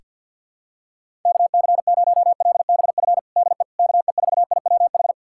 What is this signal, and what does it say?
Morse code, heard as a keyed tone, about 52 wpm, 700 Hz. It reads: KC0LBF DE KE4IWH
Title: USA Callsign DE Callsign - 6 Characters - 50wpm